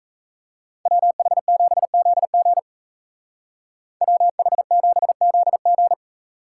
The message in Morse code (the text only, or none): WH7ZG WH7ZG